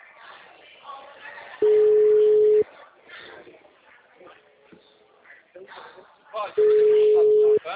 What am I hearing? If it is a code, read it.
Ringback tone (Europe, 425 Hz), with speech.